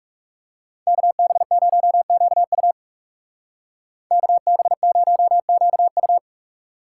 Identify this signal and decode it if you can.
Morse code, heard as a keyed tone, about 34 wpm, 700 Hz. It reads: KB0QU KB0QU
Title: Speed Racing - USA Call Signs 30wpm